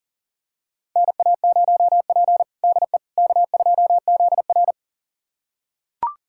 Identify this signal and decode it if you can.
Morse code, heard as a keyed tone, about 40 wpm, 700 Hz. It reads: NA0P DE K2ZR
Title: USA Callsign DE Callsign - 4 Characters - 40wpm